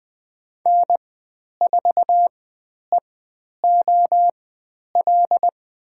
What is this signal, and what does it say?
Morse code, heard as a keyed tone, about 20 wpm, spaced as for 10 wpm, 700 Hz. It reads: N4EOL
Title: 095 Lesson 16g - Letter c - Call Signs 20wpm